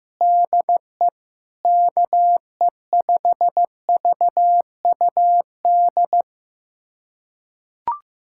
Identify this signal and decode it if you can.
Morse code, heard as a keyed tone, about 16 wpm, 700 Hz. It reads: DE KE5VUD
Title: USA Callsign DE Callsign - 6 Characters - 15wpm